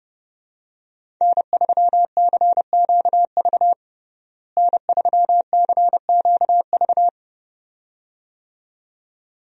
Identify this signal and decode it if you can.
Morse code, heard as a keyed tone, about 30 wpm, 700 Hz. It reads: N3CQV N3CQV
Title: USA Call Signs - 5 Characters - Repeated 2x - 30wpm